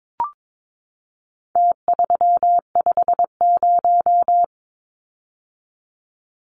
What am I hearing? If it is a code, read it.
Morse code, heard as a keyed tone, about 22 wpm, 700 Hz. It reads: T350
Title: Numbers - Contest Style - 4-Digit 22wpm